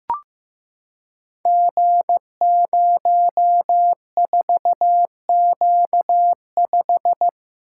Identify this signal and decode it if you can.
Morse code, heard as a keyed tone, about 15 wpm, 700 Hz. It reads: G04Q5
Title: Code Groups - 5 Letter:Numbers - 15wpm